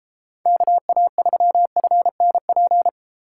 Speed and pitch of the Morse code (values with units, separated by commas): 33 wpm, 700 Hz